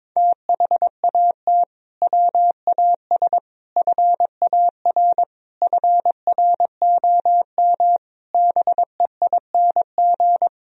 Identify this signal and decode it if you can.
Morse code, heard as a keyed tone, about 22 wpm, 700 Hz. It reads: THAT WAS FAR FROM BEING
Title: Sentences from Top 1000 Words - Review Part 3 - 22wpm